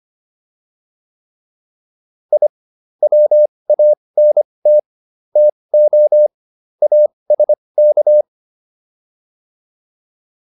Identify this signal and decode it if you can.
Morse code, heard as a keyed tone, about 25 wpm, spaced as for 20 wpm, 600 Hz. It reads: I WANT TO ASK